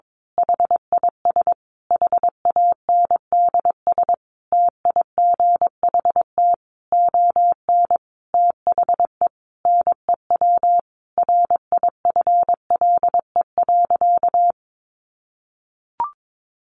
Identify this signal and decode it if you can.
Morse code, heard as a keyed tone, about 22 wpm, 700 Hz. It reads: HIS HANDS TIGHT ON THE NEW RIFLE.
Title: A Sound Of Thunder 22wpm